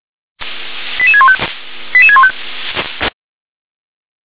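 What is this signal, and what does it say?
Five-tone selective call (ZVEI), heard as quick runs of five tones, tones of about 80 ms. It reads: 88615 88615
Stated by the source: ZVEI five-tone selective call 88786, received by SDR.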